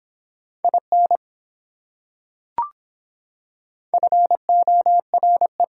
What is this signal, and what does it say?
Morse code, heard as a keyed tone, about 23 wpm, 700 Hz. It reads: IN FORE